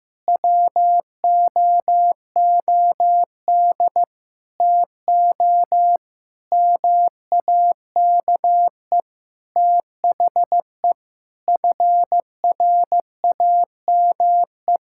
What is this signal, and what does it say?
Morse code, heard as a keyed tone, about 15 wpm, 700 Hz. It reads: WOOD TO MAKE THE FRAME
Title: Sentences from Top 1000 Words - Review Part 6 - 15wpm